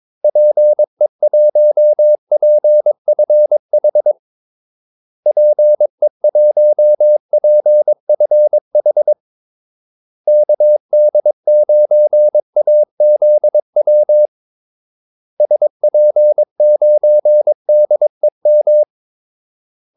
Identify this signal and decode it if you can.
Morse code, heard as a keyed tone, about 22 wpm, 600 Hz. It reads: PE1PFH PE1PFH KD9AZW SP9DEM
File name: Callsigns-22wpm